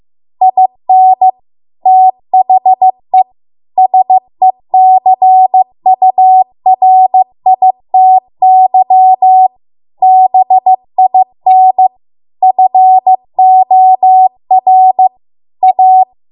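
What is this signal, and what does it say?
Morse code, heard as a keyed tone, about 15 wpm, 750 Hz. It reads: IN THE SECURITY BIN FOR A
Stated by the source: Morse code practice at 15 wpm from the national amateur radio association's code archive.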